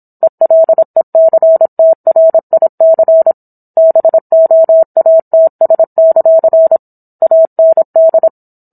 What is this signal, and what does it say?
Morse code, heard as a keyed tone, about 27 wpm, 650 Hz. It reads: ELECTRIC BOATS; AND